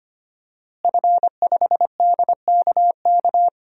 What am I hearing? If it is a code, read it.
Morse code, heard as a keyed tone, about 25 wpm, 700 Hz. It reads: F5DKK
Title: DXCC-100 Call Signs - 5 Characters - Repeated 3x - 25wpm